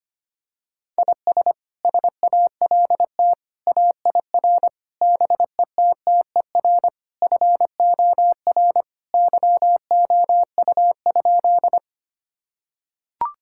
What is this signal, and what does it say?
Morse code, heard as a keyed tone, about 25 wpm, 700 Hz. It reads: IS SALT AIR BETTER FOR YOU?
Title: Sentences from Top 1000 Words - Review Part 5 - 25wpm